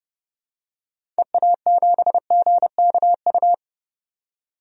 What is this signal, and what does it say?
Morse code, heard as a keyed tone, about 30 wpm, 690 Hz. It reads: EA7GKU
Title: DXCC-100 Call Signs - 6 Characters - 30wpm